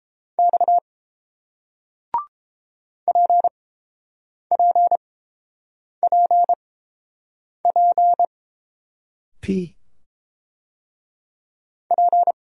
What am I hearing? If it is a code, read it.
Morse code, heard as a keyed tone, about 30 wpm, spaced as for 7 wpm, 700 Hz. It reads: X PPPP P